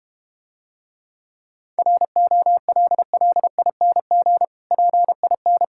Morse code, heard as a keyed tone, about 32 wpm, 700 Hz. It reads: ROLLING PIN